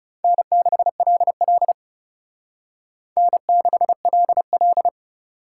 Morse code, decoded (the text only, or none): N6LL N6LL